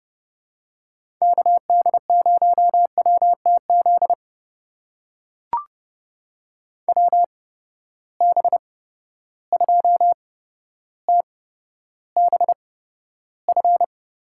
Morse code, heard as a keyed tone, about 31 wpm, spaced as for 13 wpm, 700 Hz. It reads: KD0WTZ WB2TBF
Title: USA Call Signs - 6 Characters - 8x Character Spacing - 30wpm